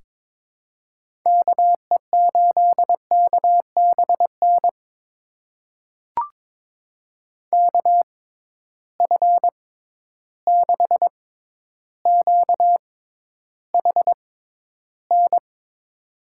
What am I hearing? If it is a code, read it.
Morse code, heard as a keyed tone, about 22 wpm, spaced as for 11 wpm, 700 Hz. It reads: KE8KBN KF6QHN
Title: USA Call Signs - 6 Characters - 6x Character Spacing - 22wpm